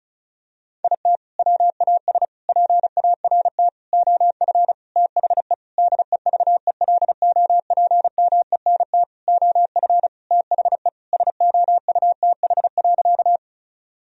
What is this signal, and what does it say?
Morse code, heard as a keyed tone, about 34 wpm, 700 Hz. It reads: IT WAS PART OF THE DEVELOPMENT OF THE SOUTH.
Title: Sentences from Top 1000 Words - Review Part 4 - 35wpm